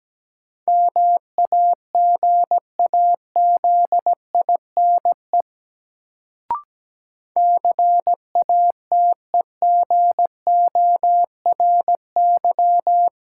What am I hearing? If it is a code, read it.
Morse code, heard as a keyed tone, about 17 wpm, 700 Hz. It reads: MAGAZINE CATEGORY
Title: Sentences from Top 1000 Words - Review Part 7 - New Word Drill - 17wpm